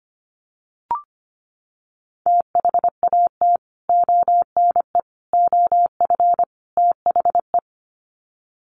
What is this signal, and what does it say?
Morse code, heard as a keyed tone, about 25 wpm, 700 Hz. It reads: THAT ONE OF THE
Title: Sets of 4 Words 25wpm